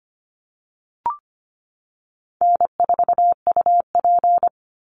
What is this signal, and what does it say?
Morse code, heard as a keyed tone, about 25 wpm, 700 Hz. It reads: N4UP
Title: Call Signs - 25wpm